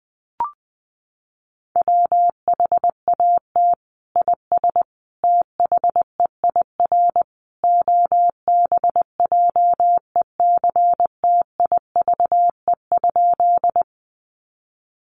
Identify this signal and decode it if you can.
Morse code, heard as a keyed tone, about 20 wpm, 700 Hz. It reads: WHAT IS THEIR OBJECTIVE?
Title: Sentences from Top 1200 Words 20wpm